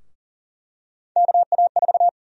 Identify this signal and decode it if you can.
Morse code, heard as a keyed tone, about 40 wpm, 700 Hz. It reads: KA4
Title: USA Call Sign Prefixes - Repeated 2x - 40wpm